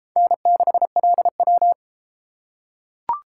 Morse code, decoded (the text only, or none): N6LW